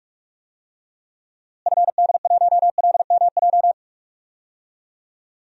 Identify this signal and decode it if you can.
Morse code, heard as a keyed tone, about 42 wpm, 700 Hz. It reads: FD1LMJ
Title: DXCC-100 Call Signs - 6 Characters - Repeated 2x - 45wpm